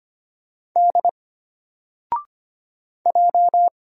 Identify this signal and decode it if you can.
Morse code, heard as a keyed tone, about 25 wpm, 700 Hz. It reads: D J
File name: Single-Letter-Number-25wpm